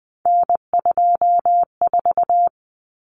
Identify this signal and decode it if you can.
Morse code, heard as a keyed tone, about 20 wpm, 700 Hz. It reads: N24
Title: Common Cut Numbers - 3-Digit 20wpm